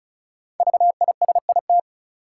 Morse code, heard as a keyed tone, about 34 wpm, 700 Hz. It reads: VISIT
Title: Sentences from Top 900 Words - New Word Drill - 35wpm